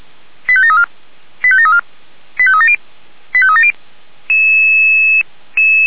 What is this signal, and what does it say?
Five-tone selective call (ZVEI), heard as quick runs of five tones, tones of about 70 ms. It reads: 85734 85734 85370 85370